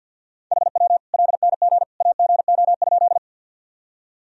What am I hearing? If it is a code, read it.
Morse code, heard as a keyed tone, about 46 wpm, 700 Hz. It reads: HW LNG AGO?
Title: Sentences from Top 500 Words and Abbreviations 50wpm